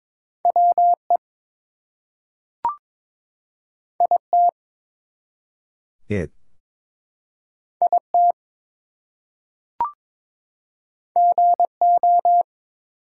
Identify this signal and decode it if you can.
Morse code, heard as a keyed tone, about 22 wpm, 700 Hz. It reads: WE IT IT GO
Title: Two Letter Words - 22wpm